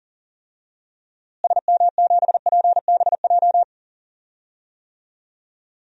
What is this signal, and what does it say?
Morse code, heard as a keyed tone, about 40 wpm, 700 Hz. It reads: SM7PBJ
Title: DXCC-100 Call Signs - 6 Characters - Repeated 2x - 40wpm